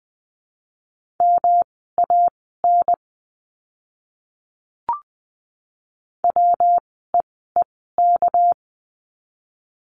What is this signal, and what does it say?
Morse code, heard as a keyed tone, about 20 wpm, spaced as for 15 wpm, 700 Hz. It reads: MAN WEEK
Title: 230 Lesson 37f - Letter x - Words 20wpm